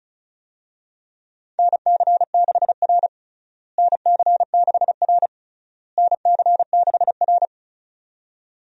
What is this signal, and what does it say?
Morse code, heard as a keyed tone, about 36 wpm, 700 Hz. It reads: NC6R NC6R NC6R
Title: USA Call Signs - 4 Characters - Repeated 3x - 35wpm